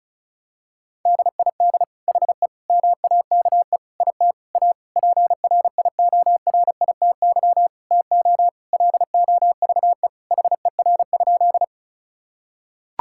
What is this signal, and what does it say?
Morse code, heard as a keyed tone, about 36 wpm, 700 Hz. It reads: DID HE MAKE IT A PRIORITY TO LOVE HER?
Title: Sentences from Top 1600 Words 35wpm